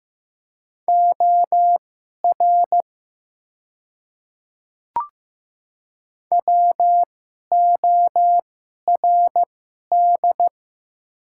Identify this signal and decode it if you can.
Morse code, heard as a keyed tone, about 15 wpm, spaced as for 11 wpm, 700 Hz. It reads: OR WORD
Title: 2x Character Spacing - Top 200 Words - 15wpm